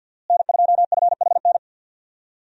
Morse code, heard as a keyed tone, about 52 wpm, 700 Hz. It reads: N2FHN
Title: USA Call Signs - 5 Characters - Repeated 2x - 50wpm